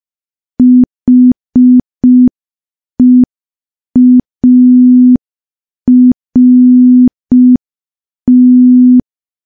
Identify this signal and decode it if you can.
Morse code, heard as a keyed tone, about 5 wpm, 250 Hz. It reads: HEART
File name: morse / morse-training-05WPM